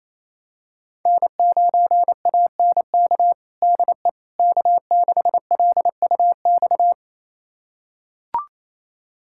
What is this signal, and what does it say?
Morse code, heard as a keyed tone, about 27 wpm, 700 Hz. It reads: N9ANK DE K6LUX
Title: USA Callsign DE Callsign 28wpm